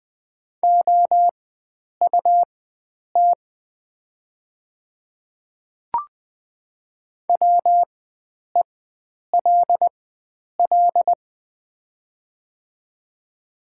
Morse code, heard as a keyed tone, about 20 wpm, spaced as for 9 wpm, 700 Hz. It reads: OUT WELL
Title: 4x Character Spacing - Top 300 Words - 20wpm